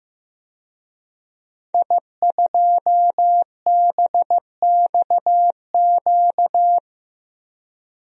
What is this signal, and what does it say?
Morse code, heard as a keyed tone, about 15 wpm, 700 Hz. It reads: I2BXQ
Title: DXCC-100 Call Signs - 5 Characters - Repeated 3x - 15wpm